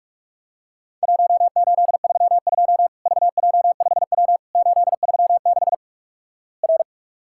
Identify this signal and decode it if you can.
Morse code, heard as a keyed tone, about 46 wpm, 700 Hz. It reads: 1832 VJ5W 836 R
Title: Worked All Europe DX Contest - QTC Exchange - 45wpm